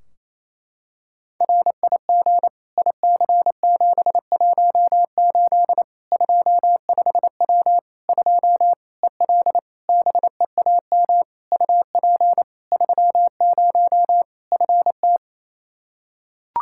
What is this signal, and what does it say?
Morse code, heard as a keyed tone, about 28 wpm, 700 Hz. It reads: RIG IC718 25W 2 EL BEAM UP 30 FT
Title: Practice QSO - Exchange 2 - Info - 28wpm